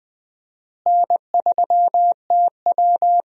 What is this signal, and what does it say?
Morse code, heard as a keyed tone, about 20 wpm, 700 Hz. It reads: N3TW
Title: USA Call Signs - 4 Characters - Repeated 3x - 20wpm